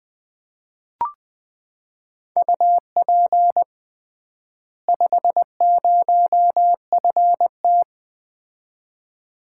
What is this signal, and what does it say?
Morse code, heard as a keyed tone, about 20 wpm, 700 Hz. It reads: UP 50FT
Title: Common QSO Phrases - 3x Word-Spacing - 20wpm